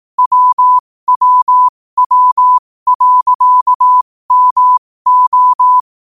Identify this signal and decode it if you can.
Morse code, heard as a keyed tone, about 18 wpm, spaced as for 15 wpm, 1000 Hz. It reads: WWW.MO
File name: qotdfiles / 15wpm